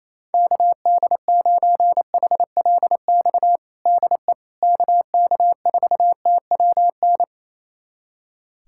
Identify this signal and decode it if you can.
Morse code, heard as a keyed tone, about 28 wpm, 700 Hz. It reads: KD9HLX DE KK4TWN
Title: USA Callsign DE Callsign - 6 Characters - 28wpm